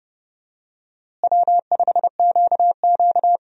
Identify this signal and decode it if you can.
Morse code, heard as a keyed tone, about 30 wpm, 700 Hz. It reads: W5QQ
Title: USA Call Signs - 4 Characters - Repeated 3x - 30wpm